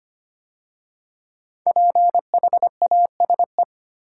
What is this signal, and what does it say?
Morse code, heard as a keyed tone, about 25 wpm, 700 Hz. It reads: PHASE
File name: 5 Letter Words - First 3 Letters Given - 25wpm